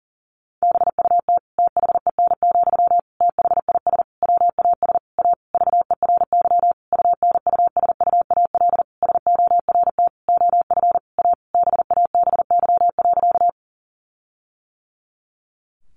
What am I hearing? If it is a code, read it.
Morse code, heard as a keyed tone, about 40 wpm, 700 Hz. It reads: BUT THEN, THIS WAS A VERY UNUSUAL SORT OF A BABY.
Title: Sentences from Top 1400 Words 40wpm